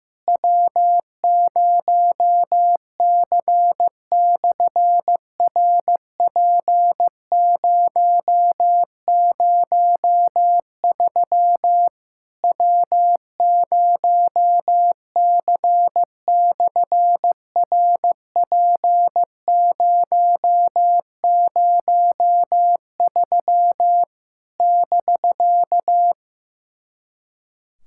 Morse code, heard as a keyed tone, about 15 wpm, 700 Hz. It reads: W0C/RP003 W0C/RP003 <BK>
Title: SOTA QSO Practice 15wpm